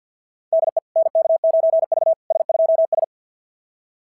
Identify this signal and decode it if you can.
Morse code, heard as a keyed tone, about 53 wpm, 690 Hz. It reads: DE NK9V S2S
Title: SOTA QSO Practice 50wpm